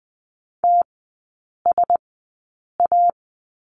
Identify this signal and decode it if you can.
Morse code, heard as a keyed tone, about 20 wpm, spaced as for 8 wpm, 700 Hz. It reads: TSA
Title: 041 Lesson 8e - Number 1 - Three Characters 20wpm